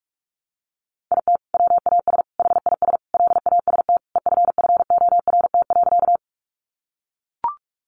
Morse code, heard as a keyed tone, about 44 wpm, 700 Hz. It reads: IT WAS HIS LAST EFFORT.
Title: Sentences from Top 1200 Words 45wpm